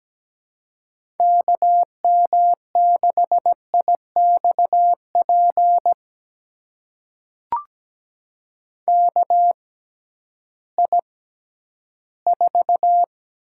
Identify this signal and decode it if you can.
Morse code, heard as a keyed tone, about 17 wpm, spaced as for 11 wpm, 700 Hz. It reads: KM6IXP KI4
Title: USA Call Signs - 6 Characters - 6x Character Spacing - 17wpm